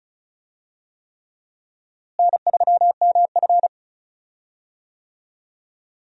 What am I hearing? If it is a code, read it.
Morse code, heard as a keyed tone, about 36 wpm, 700 Hz. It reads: N3MF
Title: USA Call Signs - 4 Characters - Repeated 3x - 35wpm